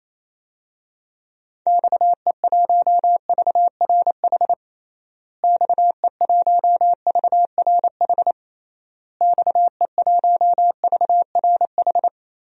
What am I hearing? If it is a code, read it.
Morse code, heard as a keyed tone, about 28 wpm, 700 Hz. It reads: XE1VRH XE1VRH XE1VRH
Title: DXCC-100 Call Signs - 6 Characters - Repeated 3x - 28wpm